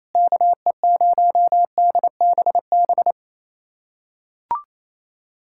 Morse code, heard as a keyed tone, about 28 wpm, 700 Hz. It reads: KE0DBB
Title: USA Callsign DE Callsign - 6 Characters - 28wpm